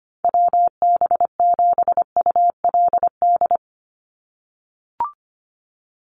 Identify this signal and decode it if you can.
Morse code, heard as a keyed tone, about 25 wpm, 700 Hz. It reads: WB7ULD
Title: NA SOTA Chaser Callsigns 25wpm